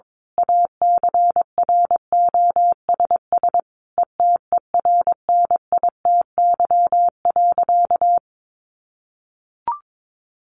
Morse code, heard as a keyed tone, about 22 wpm, 700 Hz. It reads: ACROSS ETERNITY.
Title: A Sound Of Thunder 22wpm